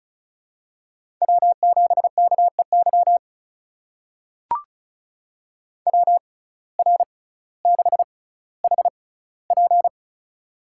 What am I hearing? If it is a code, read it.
Morse code, heard as a keyed tone, about 34 wpm, spaced as for 18 wpm, 700 Hz. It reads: W7KEY WR6HP
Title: USA Call Signs - 5 Characters - 6x Character Spacing - 35wpm